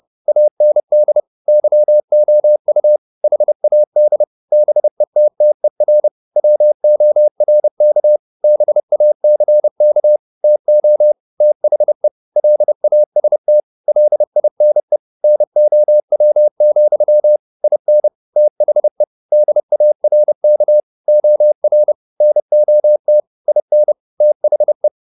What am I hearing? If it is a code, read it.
Morse code, heard as a keyed tone, about 30 wpm, 600 Hz. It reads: AND YOU HAD BETTER WORK BACK TO THE LAST LINE NOW, IN THE DARK OR NOT IN THE
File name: the old man and the sea / the old man and the sea 1 30wpm